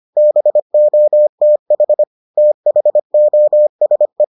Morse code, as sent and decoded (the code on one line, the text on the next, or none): -... --- - .... / - .... --- ... .
BOTH THOSE